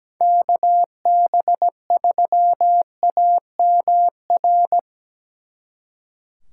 Morse code, heard as a keyed tone, about 19 wpm, 700 Hz. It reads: KB3AMR
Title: Speed Racing - USA 2x3 Call Signs 17wpm